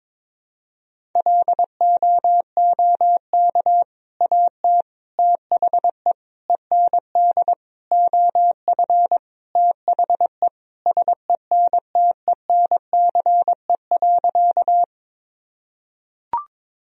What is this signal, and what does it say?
Morse code, heard as a keyed tone, about 22 wpm, 700 Hz. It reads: LOOK AT THE END OF THE SENTENCE.
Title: Sentences from Top 1000 Words - Review Part 1 - 22wpm